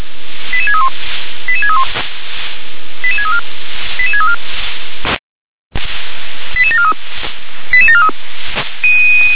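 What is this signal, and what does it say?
Five-tone selective call (ZVEI), heard as quick runs of five tones, tones of about 70 ms. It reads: 88631 88631 88634 88634 88743 88743
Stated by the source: ZVEI five-tone selective call 88743 received via SDR.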